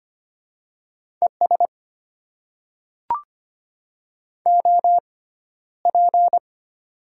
Morse code, heard as a keyed tone, about 25 wpm, spaced as for 13 wpm, 700 Hz. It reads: ES OP
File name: Top 90 QSO Elements - 6x Character Spacing - 25wpm